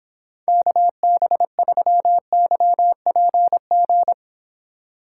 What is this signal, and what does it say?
Morse code, heard as a keyed tone, about 26 wpm, 700 Hz. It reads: KB3YPG